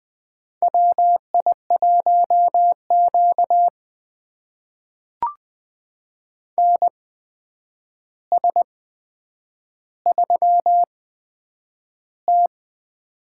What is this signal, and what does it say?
Morse code, heard as a keyed tone, about 20 wpm, spaced as for 9 wpm, 700 Hz. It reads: WI1Q NS3T
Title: USA Call Signs - 4 Characters - 8x Character Spacing - 20wpm